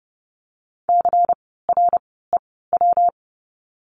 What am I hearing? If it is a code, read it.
Morse code, heard as a keyed tone, about 30 wpm, 700 Hz. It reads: C R E W
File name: course / 30:10wpm / 115 Lesson 19f - Letter w - Words 30wpm